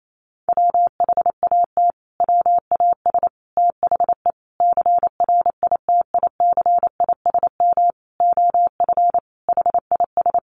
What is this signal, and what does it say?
Morse code, heard as a keyed tone, about 28 wpm, 700 Hz. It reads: WHAT WAS THE CRITICISM OF HIS
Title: Sentences from Top 1300 Words 28wpm